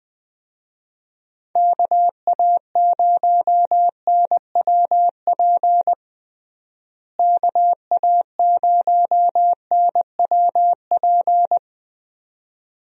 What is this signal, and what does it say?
Morse code, heard as a keyed tone, about 20 wpm, 700 Hz. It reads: KA0NWP KA0NWP
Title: USA Call Signs - 6 Characters - Repeated 2x - 20wpm